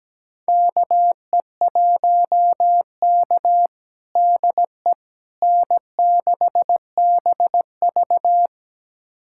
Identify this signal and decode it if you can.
Morse code, heard as a keyed tone, about 17 wpm, 700 Hz. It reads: KE1K DE N6BV
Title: USA Callsign DE Callsign - 4 Characters - 17wpm